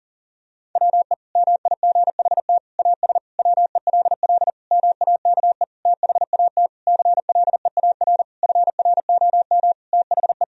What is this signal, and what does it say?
Morse code, heard as a keyed tone, about 40 wpm, 700 Hz. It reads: WE MIGHT AS WELL MAKE THAT CLEAR FROM THE
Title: Sentences from Top 1000 Words - Review Part 1 - 40wpm